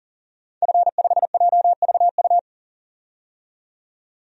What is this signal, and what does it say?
Morse code, heard as a keyed tone, about 40 wpm, 700 Hz. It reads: F5JVU
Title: DXCC-100 Call Signs - 5 Characters - 40wpm